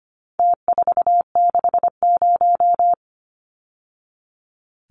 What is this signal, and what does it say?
Morse code, heard as a keyed tone, about 25 wpm, 700 Hz. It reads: T460
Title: Numbers - Contest Style - 4-Digit 25wpm